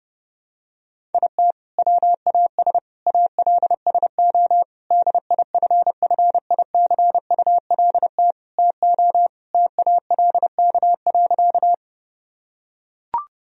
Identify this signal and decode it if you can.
Morse code, heard as a keyed tone, about 32 wpm, 700 Hz. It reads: IT WAS ALSO DIFFICULT TO TALK.
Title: Sentences from Top 1000 Words - Review Part 1 - 30wpm